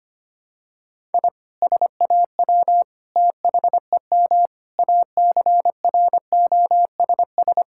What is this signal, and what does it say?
Morse code, heard as a keyed tone, about 25 wpm, 700 Hz. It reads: I SAW THEM ACROSS
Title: Sentences from Top 1000 Words - Review Part 5 - 25wpm